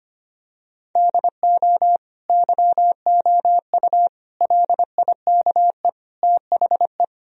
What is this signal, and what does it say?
Morse code, heard as a keyed tone, about 25 wpm, 700 Hz. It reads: DO YOU LIKE THE
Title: Sentences from Top 1000 Words - Review Part 4 - 25wpm